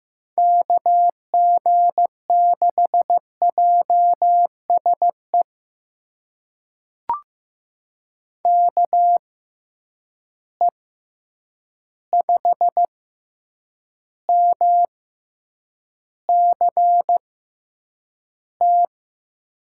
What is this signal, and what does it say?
Morse code, heard as a keyed tone, about 15 wpm, spaced as for 8 wpm, 700 Hz. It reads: KG6JSE KE5MCT
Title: USA Call Signs - 6 Characters - 6x Character Spacing - 15wpm